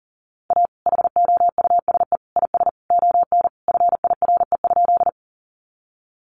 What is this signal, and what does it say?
Morse code, heard as a keyed tone, about 40 wpm, 700 Hz. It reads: A HOUSE IS ON FIRE?
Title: Sentences from Top 400 Words 40wpm